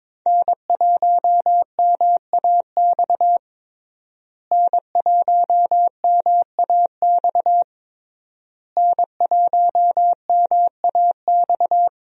Morse code, heard as a keyed tone, about 22 wpm, 700 Hz. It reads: N1MAX N1MAX N1MAX
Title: USA Call Signs - 5 Characters - Repeated 3x - 22wpm